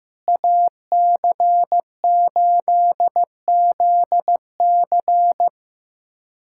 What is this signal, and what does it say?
Morse code, heard as a keyed tone, about 15 wpm, 700 Hz. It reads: AC8ZC
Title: USA Callsign DE Callsign - 5 Characters - 15wpm